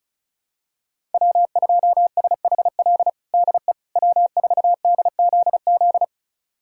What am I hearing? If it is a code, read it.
Morse code, heard as a keyed tone, about 34 wpm, 700 Hz. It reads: W2SHL DE W4DZZ
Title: USA Callsign DE Callsign - 5 Characters - 35wpm